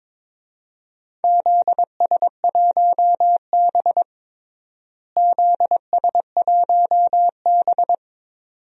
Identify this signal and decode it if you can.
Morse code, heard as a keyed tone, about 21 wpm, 700 Hz. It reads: ZS1B ZS1B
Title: DXCC-100 Call Signs - 4 Characters - Repeated 2x - 22wpm